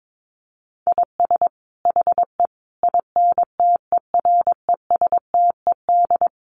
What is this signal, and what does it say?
Morse code, heard as a keyed tone, about 22 wpm, 700 Hz. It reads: IS HE INTERESTED
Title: Sentences from Top 900 Words 22wpm